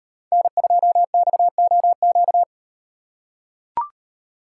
Morse code, heard as a keyed tone, about 38 wpm, 700 Hz. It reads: N2XOQ